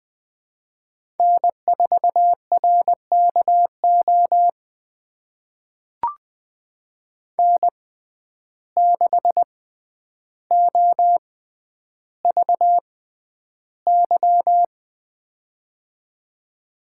Morse code, heard as a keyed tone, about 20 wpm, spaced as for 10 wpm, 700 Hz. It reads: N4RKO N6OVY
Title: USA Call Signs - 5 Characters - 6x Character Spacing - 20wpm